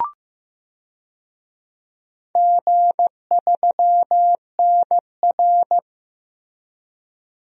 Morse code, heard as a keyed tone, about 15 wpm, 700 Hz. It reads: G3NR
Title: DXCC-100 Call Signs - 4 Characters - Repeated 3x - 15wpm